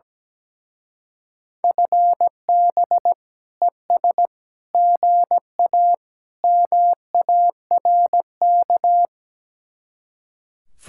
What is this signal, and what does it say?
Morse code, heard as a keyed tone, about 17 wpm, 700 Hz. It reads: FB ES GA MARK
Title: Speed-Racing Common QSO Phrases Encore - 17wpm